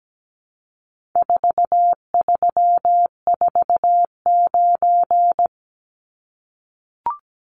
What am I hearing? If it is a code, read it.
Morse code, heard as a keyed tone, about 17 wpm, 700 Hz. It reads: 4349
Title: Numbers - 4-Digit 17wpm